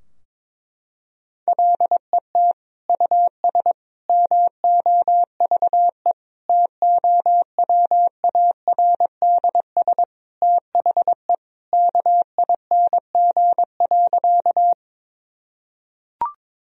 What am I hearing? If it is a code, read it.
Morse code, heard as a keyed tone, about 22 wpm, 700 Hz. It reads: LET US MOVE TOWARDS THE KING.
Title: Sentences from Top 1000 Words - Review Part 2 - 22wpm